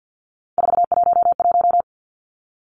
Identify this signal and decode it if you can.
Morse code, heard as a keyed tone, about 50 wpm, 700 Hz. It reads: 411
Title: Numbers - 3-Digit 50wpm